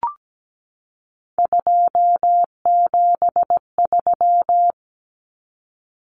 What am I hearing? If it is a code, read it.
Morse code, heard as a keyed tone, about 17 wpm, 700 Hz. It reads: 273
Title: Numbers - 3-Digit 17wpm